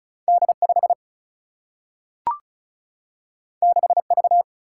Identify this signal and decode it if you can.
Morse code, heard as a keyed tone, about 36 wpm, 700 Hz. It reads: D 5 6 V
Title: Single-Letter Single-Number 35wpm